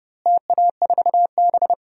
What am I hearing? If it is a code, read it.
Morse code, heard as a keyed tone, about 32 wpm, 700 Hz. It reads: TA4B
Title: DXCC-100 Call Signs - 4 Characters - Repeated 2x - 30wpm